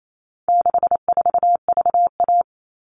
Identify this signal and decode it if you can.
Morse code, heard as a keyed tone, about 28 wpm, 700 Hz. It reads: 64VA
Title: All Cut Numbers - 4-Digit 28wpm